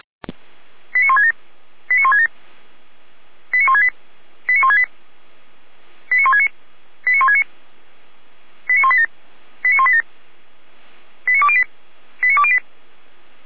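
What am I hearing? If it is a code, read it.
Five-tone selective call (ZVEI), heard as quick runs of five tones, tones of about 80 ms. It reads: 89167 89167 89168 89168 89169 89169 89187 89187 89208 89208
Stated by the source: ZVEI five-tone call received by SDR.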